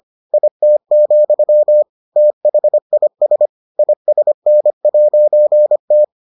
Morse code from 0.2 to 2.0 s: IT,

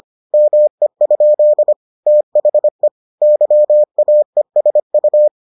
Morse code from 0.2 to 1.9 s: ME?

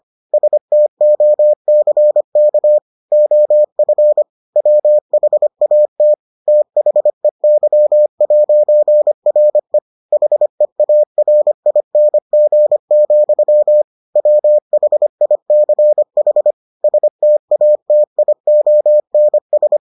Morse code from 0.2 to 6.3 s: STOCK OF WHAT

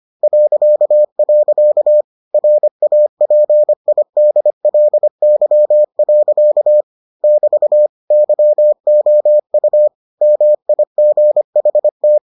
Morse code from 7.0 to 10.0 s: <BT> YOU

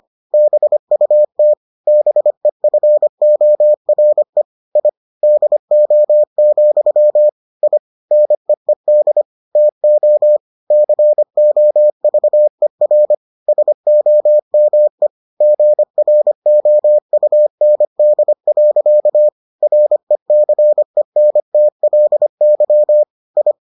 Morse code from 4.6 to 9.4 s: I DO, I NEED

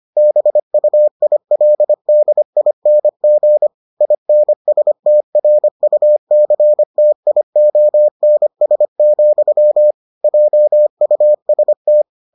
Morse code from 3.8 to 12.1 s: INSTRUCTIONS, JUST

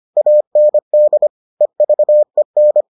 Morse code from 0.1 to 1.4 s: AND

